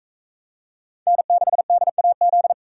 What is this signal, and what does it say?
Morse code, heard as a keyed tone, about 44 wpm, 700 Hz. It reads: N6DAZ